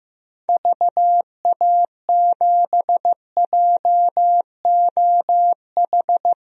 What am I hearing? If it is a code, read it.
Morse code, heard as a keyed tone, about 15 wpm, 700 Hz. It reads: VA7JOH
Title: DXCC-100 Call Signs - 6 Characters - Repeated 2x - 15wpm